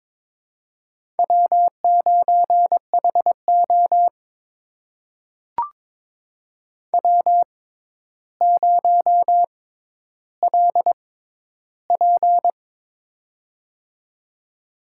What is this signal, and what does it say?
Morse code, heard as a keyed tone, about 22 wpm, spaced as for 11 wpm, 700 Hz. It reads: W9HO W0LP
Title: USA Call Signs - 4 Characters - 6x Character Spacing - 22wpm